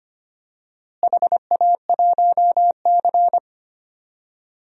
Morse code, heard as a keyed tone, about 25 wpm, 700 Hz. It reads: HA1C
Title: DXCC-100 Call Signs - 4 Characters - Repeated 3x - 25wpm